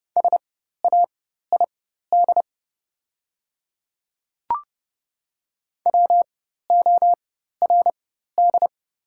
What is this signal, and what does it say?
Morse code, heard as a keyed tone, about 30 wpm, spaced as for 14 wpm, 690 Hz. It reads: SAID WORD
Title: 4x Character Spacing - Top 200 Words - 30wpm